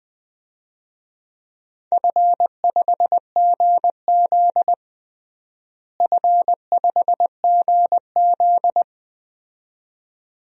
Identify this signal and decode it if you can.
Morse code, heard as a keyed tone, about 20 wpm, 700 Hz. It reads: F5GZ F5GZ
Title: DXCC-100 Call Signs - 4 Characters - Repeated 2x - 20wpm